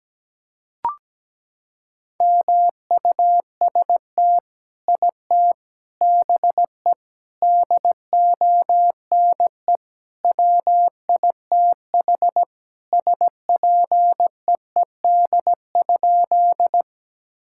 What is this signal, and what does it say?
Morse code, heard as a keyed tone, about 17 wpm, 700 Hz. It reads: MUST IT BE DONE WITH SPEED?
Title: Sentences from Top 1000 Words - Review Part 5 - 17wpm